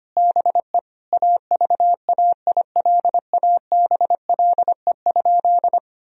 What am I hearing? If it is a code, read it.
Morse code, heard as a keyed tone, about 25 wpm, 700 Hz. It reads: BE AVAILABLE?
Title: Sentences from Top 1000 Words - Review Part 3 - 25wpm